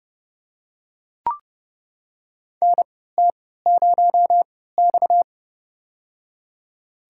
Morse code, heard as a keyed tone, about 28 wpm, spaced as for 17 wpm, 700 Hz. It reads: NT0X
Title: USA Call Signs - 4 Characters - 3x Character Spacing - 30wpm